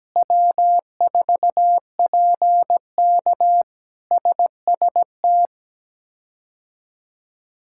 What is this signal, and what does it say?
Morse code, heard as a keyed tone, about 17 wpm, 700 Hz. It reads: W4PK SST
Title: K1USN SST QSO Practice - 17wpm